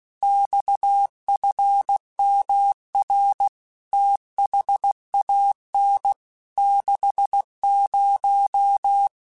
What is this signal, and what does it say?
Morse code, heard as a keyed tone, about 16 wpm, 800 Hz. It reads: XFMR THAN 60
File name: Technical Subjects / Technical Subjects-mp3s / Sw Pwr Sup @15